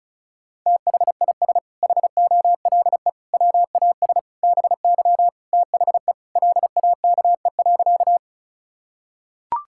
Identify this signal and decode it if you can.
Morse code, heard as a keyed tone, about 36 wpm, 700 Hz. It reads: THIS HOLE WAS BY THE LAKE.
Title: Sentences from Top 1600 Words 35wpm